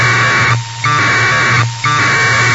Packet radio (AX.25, as in APRS).
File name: Sohla-1 1200bd packet HiMode by ja0caw